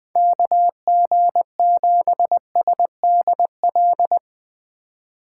Morse code, heard as a keyed tone, about 20 wpm, 700 Hz. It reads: KG7SDL
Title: Speed Racing - USA Call Signs 20wpm